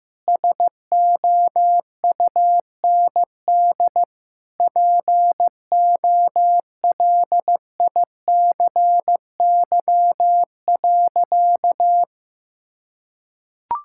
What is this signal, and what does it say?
Morse code, heard as a keyed tone, about 15 wpm, 700 Hz. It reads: SOUND POLICY.
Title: Sentences from Top 1000 Words - Review Part 5 - 15wpm